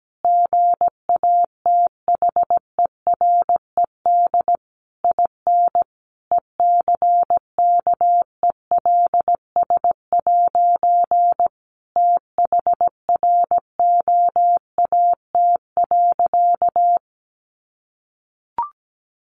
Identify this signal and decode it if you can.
Morse code, heard as a keyed tone, about 17 wpm, 700 Hz. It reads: GATHERED IN ECKELS' THROAT.
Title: A Sound Of Thunder 17wpm